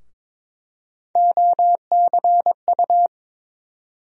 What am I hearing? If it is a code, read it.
Morse code, heard as a keyed tone, about 22 wpm, 700 Hz. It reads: OCU
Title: USA Call Sign Postfixes - 3 Letters 22wpm